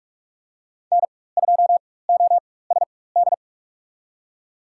Morse code, heard as a keyed tone, about 42 wpm, spaced as for 21 wpm, 700 Hz. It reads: N2OSD
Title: USA Call Signs - 5 Characters - 4x Character Spacing - 45wpm